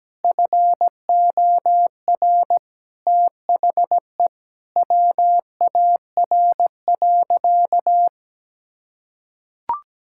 Morse code, heard as a keyed tone, about 17 wpm, 700 Hz. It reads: FOR THE WAR.
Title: Sentences from Top 1000 Words - Review Part 5 - 17wpm